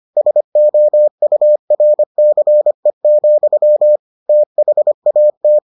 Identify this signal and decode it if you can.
Morse code, heard as a keyed tone, about 25 wpm, 600 Hz. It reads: SOURCE, THAT